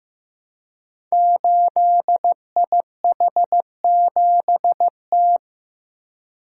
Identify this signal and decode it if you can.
Morse code, heard as a keyed tone, about 15 wpm, 700 Hz. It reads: 8IH7T
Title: Code Groups - 5 Letter:Numbers - 15wpm